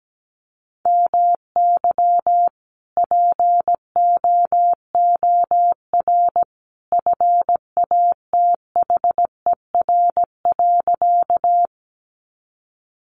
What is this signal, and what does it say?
Morse code, heard as a keyed tone, about 17 wpm, 700 Hz. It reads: MY POOR FATHER.
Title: Sentences from Top 1000 Words 17wpm